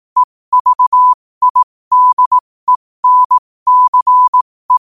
Morse code, heard as a keyed tone, about 18 wpm, spaced as for 15 wpm, 1000 Hz. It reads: EVIDENCE
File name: qotdfiles / 15wpm